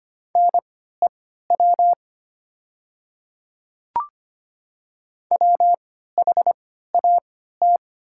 Morse code, heard as a keyed tone, about 25 wpm, spaced as for 14 wpm, 700 Hz. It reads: NEW WHAT
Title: 3x Character Spacing - Top 300 Words - 25wpm